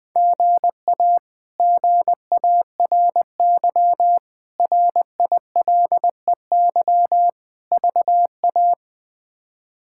Morse code, heard as a keyed tone, about 21 wpm, 700 Hz. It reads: GA GARY RILEY VA
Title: K1USN SST QSO Practice - 20wpm